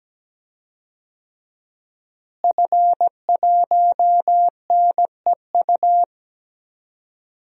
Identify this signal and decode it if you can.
Morse code, heard as a keyed tone, about 17 wpm, 700 Hz. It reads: F1NEU
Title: DXCC-100 Call Signs - 5 Characters - Repeated 2x - 17wpm